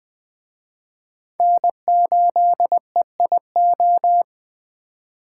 Morse code, heard as a keyed tone, about 20 wpm, 700 Hz. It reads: N8EIO
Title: Speed Racing - USA 1x3 Call Signs 17wpm